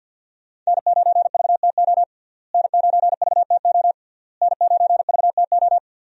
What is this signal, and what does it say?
Morse code, heard as a keyed tone, about 51 wpm, 690 Hz. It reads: N9VTO N9VTO N9VTO
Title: USA Call Signs - 5 Characters - Repeated 3x - 50wpm